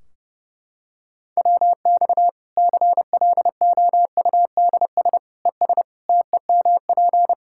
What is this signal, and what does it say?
Morse code, heard as a keyed tone, about 30 wpm, 700 Hz. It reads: WX CLOUDS ES TEMP
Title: Practice QSO - Exchange 2 - Info - 30wpm